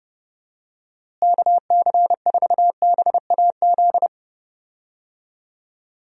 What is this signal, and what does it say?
Morse code, heard as a keyed tone, about 32 wpm, 700 Hz. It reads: KC4BAZ
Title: USA Call Signs - 6 Characters - Repeated 3x - 30wpm